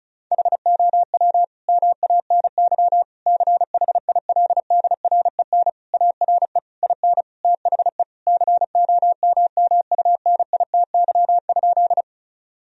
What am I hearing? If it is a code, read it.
Morse code, heard as a keyed tone, about 36 wpm, 700 Hz. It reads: HOW MANY CHILDREN ARE IN THE COMMUNITY?
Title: Sentences from Top 1000 Words - Review Part 4 - 35wpm